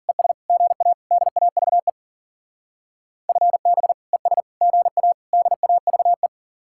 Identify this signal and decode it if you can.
Morse code, heard as a keyed tone, about 53 wpm, 700 Hz. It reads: ES GA DAVE FB ES GA DAVE